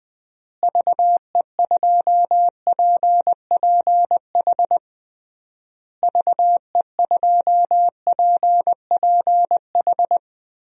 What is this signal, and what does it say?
Morse code, heard as a keyed tone, about 21 wpm, 700 Hz. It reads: VE2PPH VE2PPH
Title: DXCC-100 Call Signs - 6 Characters - Repeated 3x - 20wpm